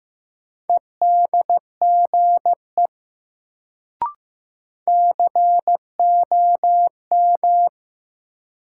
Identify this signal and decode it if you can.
Morse code, heard as a keyed tone, about 15 wpm, 700 Hz. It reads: EDGE COM
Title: Sentences from Top 1300 Words - New Word Drill Alt - 15wpm